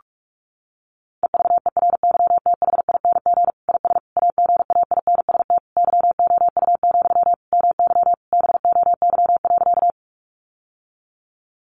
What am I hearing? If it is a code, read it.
Morse code, heard as a keyed tone, about 44 wpm, 700 Hz. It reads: EVERYTHING IS AGAINST YOU, MY BOY.
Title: Sentences from Top 700 Words 45wpm